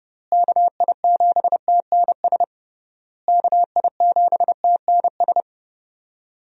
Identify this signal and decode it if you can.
Morse code, heard as a keyed tone, about 30 wpm, 700 Hz. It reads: KI7TNS KI7TNS
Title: USA Call Signs - 6 Characters - Repeated 3x - 30wpm